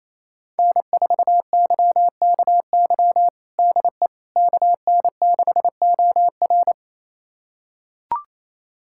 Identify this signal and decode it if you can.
Morse code, heard as a keyed tone, about 28 wpm, 700 Hz. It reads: N4YKY DE KN6OR
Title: USA Callsign DE Callsign 28wpm